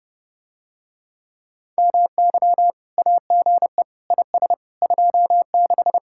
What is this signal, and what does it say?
Morse code, heard as a keyed tone, about 31 wpm, 700 Hz. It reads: MY AGE IS 26